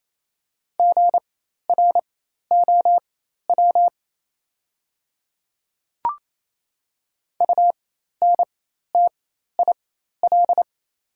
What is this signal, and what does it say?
Morse code, heard as a keyed tone, about 28 wpm, spaced as for 13 wpm, 690 Hz. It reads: GROW UNTIL
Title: 4x Character Spacing - Top 300 Words - 28wpm